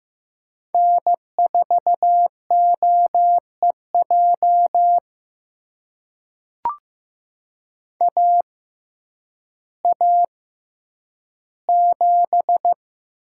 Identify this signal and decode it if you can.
Morse code, heard as a keyed tone, about 15 wpm, spaced as for 9 wpm, 700 Hz. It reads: N4OEJ AA7
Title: USA Call Signs - 5 Characters - 6x Character Spacing - 15wpm